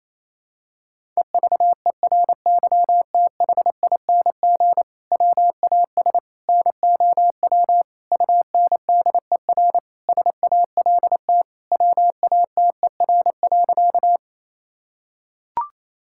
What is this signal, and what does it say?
Morse code, heard as a keyed tone, about 28 wpm, 700 Hz. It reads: EVERYTHING WAS NOW UNDER SALT WATER.
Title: Sentences from Top 1000 Words - Review Part 5 - 28wpm